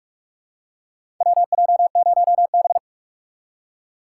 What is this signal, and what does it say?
Morse code, heard as a keyed tone, about 45 wpm, 690 Hz. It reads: WJ0B
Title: USA Call Signs - 4 Characters - 3x Character Spacing - 45wpm